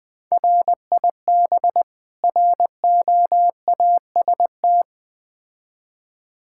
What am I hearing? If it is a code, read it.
Morse code, heard as a keyed tone, about 21 wpm, 700 Hz. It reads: RIB ROAST
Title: Difficult Words with Hints - 20wpm